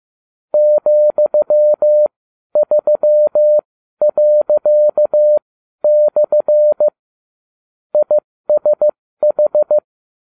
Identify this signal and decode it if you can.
Morse code, heard as a keyed tone, about 15 wpm, spaced as for 12 wpm, 600 Hz. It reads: ,3./ ISH